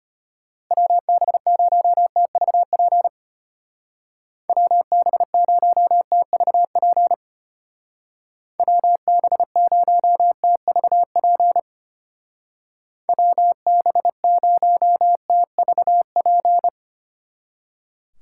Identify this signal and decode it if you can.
Morse code, heard as a keyed tone, about 31 wpm, 700 Hz. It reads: WB0TVP WB0TVP WB0TVP WB0TVP
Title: Speed Racing - USA 2x3 Call Signs 25wpm